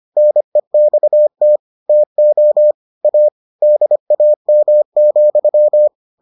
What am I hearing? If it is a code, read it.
Morse code, heard as a keyed tone, about 25 wpm, 600 Hz. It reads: NEXT TO A DAM,